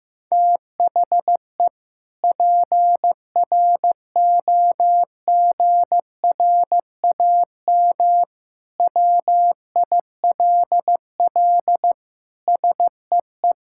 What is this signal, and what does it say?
Morse code, heard as a keyed tone, about 16 wpm, 700 Hz. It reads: THE PROGRAM WILL SEE
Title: Sentences from Top 1000 Words - Review Part 5 - 15wpm